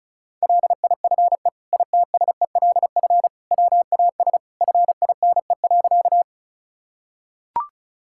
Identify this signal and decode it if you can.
Morse code, heard as a keyed tone, about 34 wpm, 700 Hz. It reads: LIFE ITSELF WAS FINE.
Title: Sentences from Top 1000 Words - Review Part 5 - 35wpm